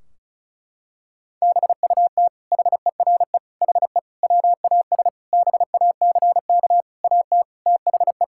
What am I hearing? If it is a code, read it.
Morse code, heard as a keyed tone, about 36 wpm, 700 Hz. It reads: BUT HERE HE WAS BACK AT THE
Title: Sentences from Top 1600 Words 35wpm